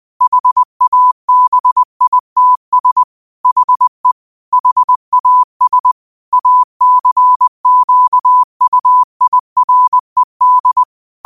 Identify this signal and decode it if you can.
Morse code, heard as a keyed tone, about 20 wpm, 1000 Hz. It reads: HABITS HE HAS ACQUIRED